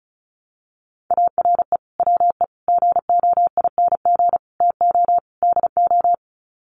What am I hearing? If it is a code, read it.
Morse code, heard as a keyed tone, about 34 wpm, 700 Hz. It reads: ARE WE GOING TO DO
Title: Sets of 5 Words 35wpm